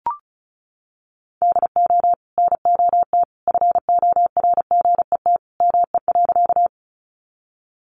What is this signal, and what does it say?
Morse code, heard as a keyed tone, about 34 wpm, 700 Hz. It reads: DO NOT FORGET ME.
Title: Sentences from Top 1200 Words 35wpm